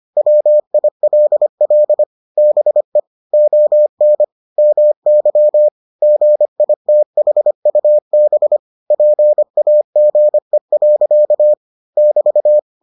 Morse code, heard as a keyed tone, about 25 wpm, 600 Hz. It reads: WILL BE ON MY GITHUB PAGE. <BT>